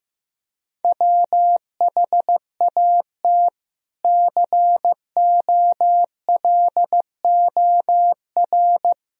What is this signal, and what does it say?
Morse code, heard as a keyed tone, about 15 wpm, 700 Hz. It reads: WHAT COLOR
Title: Sentences from Top 1000 Words - Review Part 2 - 15wpm